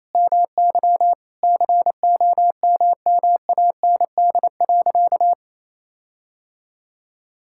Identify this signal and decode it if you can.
Morse code, heard as a keyed tone, about 28 wpm, 700 Hz. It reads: MY COMMAND.
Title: Sentences from Top 1600 Words 28wpm